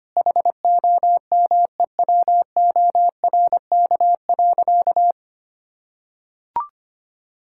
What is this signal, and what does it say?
Morse code, heard as a keyed tone, about 25 wpm, 700 Hz. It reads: HOMEWORK.
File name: practice / 25wpm / Sentences with 4-Word Phrases - 25wpm